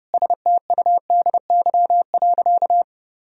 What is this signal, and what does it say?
Morse code, heard as a keyed tone, about 32 wpm, 700 Hz. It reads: STUDY.
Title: Sentences from Top 1000 Words - Review Part 1 - 30wpm